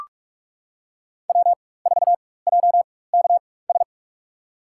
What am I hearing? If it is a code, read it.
Morse code, heard as a keyed tone, about 44 wpm, spaced as for 21 wpm, 700 Hz. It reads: W4JKS
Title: USA Call Signs - 5 Characters - 4x Character Spacing - 45wpm